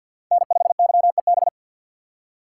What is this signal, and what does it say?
Morse code, heard as a keyed tone, about 51 wpm, 700 Hz. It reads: N5YEB
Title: USA Call Signs - 5 Characters - 6x Character Spacing - 50wpm